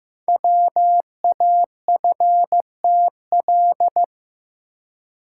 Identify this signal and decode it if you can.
Morse code, heard as a keyed tone, about 15 wpm, 700 Hz. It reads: WAFTL
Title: Code Groups - 5 Letters - 15wpm